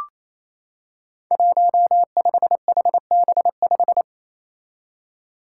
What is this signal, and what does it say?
Morse code, heard as a keyed tone, about 28 wpm, 700 Hz. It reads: 15HB5
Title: Five Challenging Letter-Number Combinations 28wpm